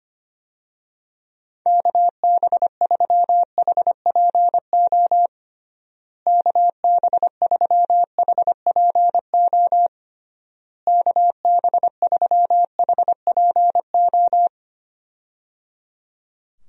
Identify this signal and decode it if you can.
Morse code, heard as a keyed tone, about 25 wpm, 700 Hz. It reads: KB3HPO KB3HPO KB3HPO
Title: USA Call Signs - 6 Characters - Repeated 3x - 25wpm